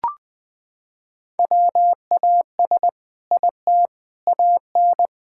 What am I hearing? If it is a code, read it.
Morse code, heard as a keyed tone, about 19 wpm, 700 Hz. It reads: WAS IT AN
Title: Sentences from Top 1000 Words - Review Part 6 - 20wpm